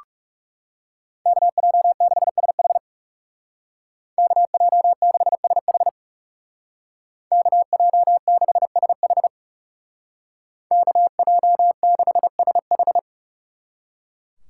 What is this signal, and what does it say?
Morse code, heard as a keyed tone, about 37 wpm, 700 Hz. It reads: KJ6SH KJ6SH KJ6SH KJ6SH
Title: Speed Racing - USA 2x2 Call Signs 30wpm